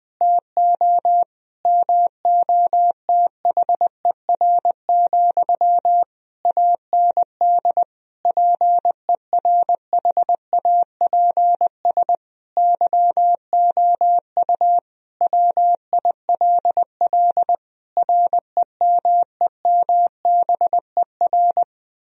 Morse code, heard as a keyed tone, about 20 wpm, 700 Hz. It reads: TO MOTHER, AND PERHAPS YOU WILL REMEMBER
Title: Sentences from Top 1000 Words - Review Part 1 - 20wpm